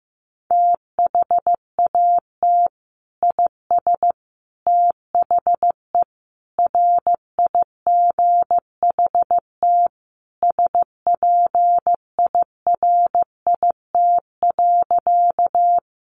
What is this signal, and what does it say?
Morse code, heard as a keyed tone, about 15 wpm, 700 Hz. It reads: THAT IS THE RIGHT SPIRIT.
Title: Sentences from Top 1400 Words 15wpm